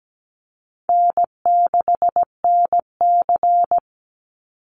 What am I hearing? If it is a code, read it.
Morse code, heard as a keyed tone, about 17 wpm, 700 Hz. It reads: N6NC
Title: Easy Call Signs Repeated 2x - 17wpm